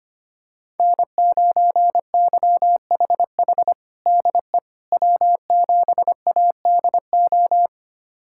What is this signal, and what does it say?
Morse code, heard as a keyed tone, about 25 wpm, 700 Hz. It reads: N9YHH DE W7ADO
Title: USA Callsign DE Callsign - 5 Characters - 25wpm